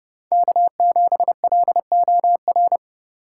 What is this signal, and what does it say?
Morse code, heard as a keyed tone, about 30 wpm, 700 Hz. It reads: K7LOR
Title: USA Call Signs - 5 Characters - 6x Character Spacing - 30wpm